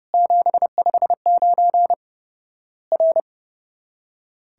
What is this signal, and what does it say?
Morse code, heard as a keyed tone, about 30 wpm, 700 Hz. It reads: 759 R
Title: Worked All Europe DX Contest - QTC Exchange - 30wpm